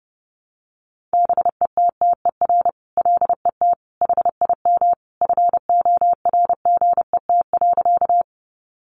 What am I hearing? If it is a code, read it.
Morse code, heard as a keyed tone, about 30 wpm, 700 Hz. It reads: BETTER LET HIM FORGET.
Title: Sentences from Top 1200 Words 30wpm